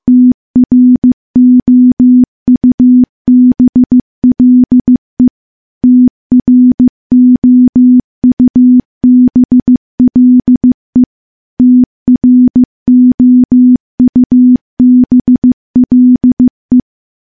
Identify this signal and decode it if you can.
Morse code, heard as a keyed tone, about 15 wpm, 250 Hz. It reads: TROUBLE TROUBLE TROUBLE
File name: morse / morse-training-15WPM